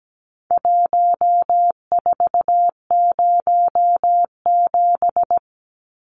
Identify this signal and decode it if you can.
Morse code, heard as a keyed tone, about 17 wpm, 700 Hz. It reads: 1407
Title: Numbers - Contest Style - 4-Digit 17wpm